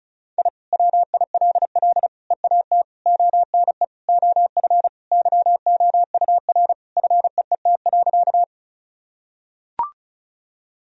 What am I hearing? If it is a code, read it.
Morse code, heard as a keyed tone, about 34 wpm, 690 Hz. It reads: I WILL EAT ONE OF YOUR FEET.
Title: Sentences from Top 1000 Words - Review Part 1 - 35wpm